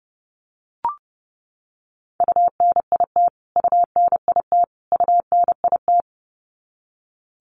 Morse code, heard as a keyed tone, about 30 wpm, 700 Hz. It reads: UNIT UNIT UNIT
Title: Top 500 Words Repeated 30wpm